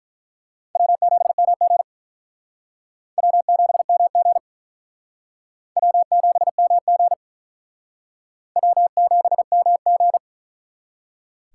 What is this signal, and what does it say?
Morse code, heard as a keyed tone, about 42 wpm, 700 Hz. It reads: W7MG W7MG W7MG W7MG
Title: Speed Racing - USA 1x2 Call Signs 35wpm